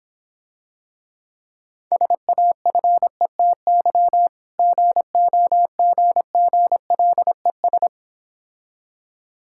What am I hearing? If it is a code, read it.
Morse code, heard as a keyed tone, about 26 wpm, 700 Hz. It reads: SAFETY GOGGLES